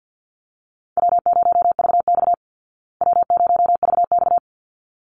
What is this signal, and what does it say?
Morse code, heard as a keyed tone, about 50 wpm, 700 Hz. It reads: W0VX W0VX
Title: Easy Call Signs Repeated 2x - 50wpm